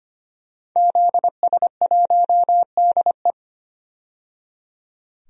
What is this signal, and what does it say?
Morse code, heard as a keyed tone, about 25 wpm, 700 Hz. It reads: ZS1DE
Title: DXCC-100 Call Signs - 5 Characters - 25wpm